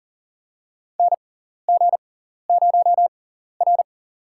Morse code, heard as a keyed tone, about 40 wpm, spaced as for 14 wpm, 700 Hz. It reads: NG0R
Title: USA Call Signs - 6x Character Spacing - 40wpm